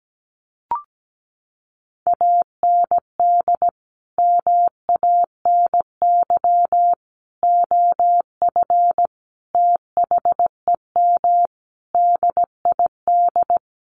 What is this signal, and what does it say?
Morse code, heard as a keyed tone, about 17 wpm, 700 Hz. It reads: AND MANY OF THEM DID
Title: Sentences from Top 1500 Words 17wpm